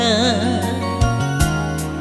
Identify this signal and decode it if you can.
There is no code.